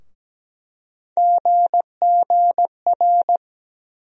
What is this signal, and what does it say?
Morse code, heard as a keyed tone, about 17 wpm, 700 Hz. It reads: GGR
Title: USA Call Sign Postfixes - 3 Letters 17wpm